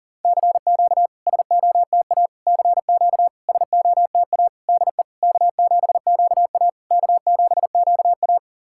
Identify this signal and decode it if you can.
Morse code, heard as a keyed tone, about 40 wpm, 690 Hz. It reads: CQ SOTA CQ SOTA DE K7QA K7QA
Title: SOTA QSO Practice 40wpm